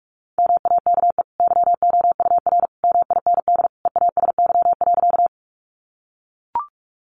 Morse code, heard as a keyed tone, about 44 wpm, 700 Hz. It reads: MAKE YOUR MIND EASY.
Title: Sentences from Top 700 Words 45wpm